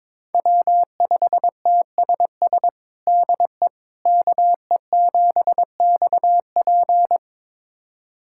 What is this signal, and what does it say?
Morse code, heard as a keyed tone, about 22 wpm, 700 Hz. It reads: W5TSS DE KE7XP
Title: USA Callsign DE Callsign 22wpm